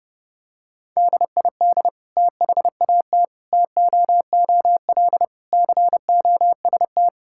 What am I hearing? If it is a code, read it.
Morse code, heard as a keyed tone, about 30 wpm, 700 Hz. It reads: DID THAT TOOL COST
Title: Sentences from Top 1000 Words - Review Part 6 - 30wpm